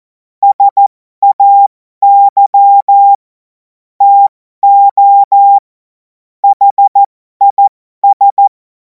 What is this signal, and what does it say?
Morse code, heard as a keyed tone, about 14 wpm, spaced as for 12 wpm, 800 Hz. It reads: SAY TO HIS
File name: Morsekurs / Witze